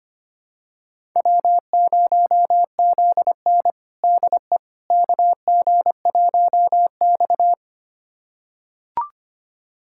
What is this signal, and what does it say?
Morse code, heard as a keyed tone, about 25 wpm, 700 Hz. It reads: W0ZN DE KG1X
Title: USA Callsign DE Callsign - 4 Characters - 25wpm